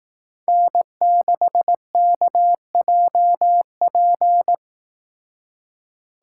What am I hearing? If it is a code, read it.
Morse code, heard as a keyed tone, about 18 wpm, 700 Hz. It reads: N6KJP
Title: Speed Racing - USA 1x3 Call Signs 15wpm